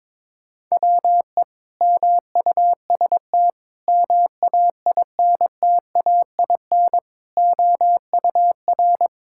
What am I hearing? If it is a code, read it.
Morse code, heard as a keyed tone, about 22 wpm, 700 Hz. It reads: WE MUST MAINTAIN OUR
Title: Sentences from Top 1000 Words - Review Part 5 - 22wpm